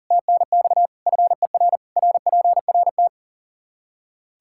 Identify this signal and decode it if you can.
Morse code, heard as a keyed tone, about 40 wpm, 700 Hz. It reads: TNX FER RPRT
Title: Practice QSO - Exchange 2 - Info - 40wpm